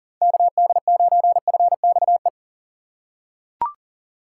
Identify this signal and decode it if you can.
Morse code, heard as a keyed tone, about 40 wpm, 700 Hz. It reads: KD9FXE
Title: USA Call Signs - 6 Characters - 3x Character Spacing - 40wpm